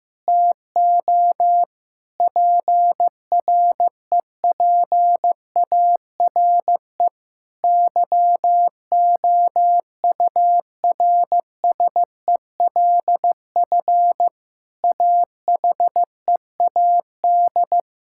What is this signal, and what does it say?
Morse code, heard as a keyed tone, about 15 wpm, 700 Hz. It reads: TO PREPARE YOURSELF AHEAD
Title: Sentences from Top 1000 Words - Review Part 6 - 15wpm